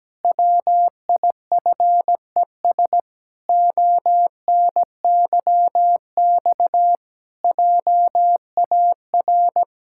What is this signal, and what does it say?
Morse code, heard as a keyed tone, about 17 wpm, 700 Hz. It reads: WIFES ONYX JAR
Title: Pangram Challenge v2 - 17wpm